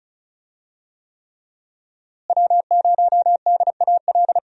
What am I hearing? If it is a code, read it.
Morse code, heard as a keyed tone, about 36 wpm, 700 Hz. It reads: W0DAL
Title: USA Call Signs - 5 Characters - Repeated 3x - 35wpm